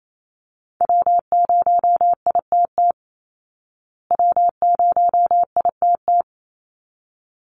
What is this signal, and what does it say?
Morse code, heard as a keyed tone, about 28 wpm, 700 Hz. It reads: W0ITT W0ITT
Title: Call Signs Repeated 3x - 28wpm